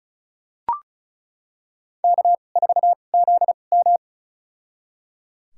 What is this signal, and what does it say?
Morse code, heard as a keyed tone, about 36 wpm, spaced as for 26 wpm, 700 Hz. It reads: K4ZM
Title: USA Call Signs - 4 Characters - 2x Character Spacing - 35wpm